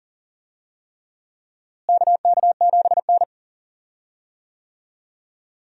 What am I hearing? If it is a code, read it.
Morse code, heard as a keyed tone, about 40 wpm, 700 Hz. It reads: KK7N
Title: USA Call Signs - 4 Characters - Repeated 3x - 40wpm